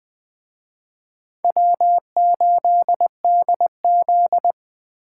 Morse code, heard as a keyed tone, about 19 wpm, 700 Hz. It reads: W8DZ